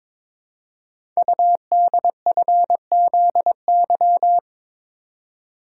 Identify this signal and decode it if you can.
Morse code, heard as a keyed tone, about 22 wpm, 700 Hz. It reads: UDFZY
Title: Code Groups - 5 Letters - 22wpm